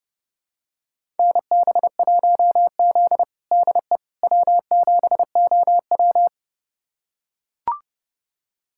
Morse code, heard as a keyed tone, about 30 wpm, 700 Hz. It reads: NB1Z DE W7OW
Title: USA Callsign DE Callsign - 4 Characters - 30wpm